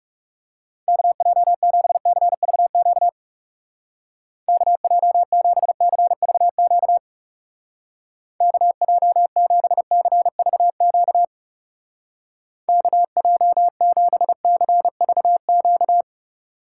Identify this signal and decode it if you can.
Morse code, heard as a keyed tone, about 39 wpm, 700 Hz. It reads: KJ7CVQ KJ7CVQ KJ7CVQ KJ7CVQ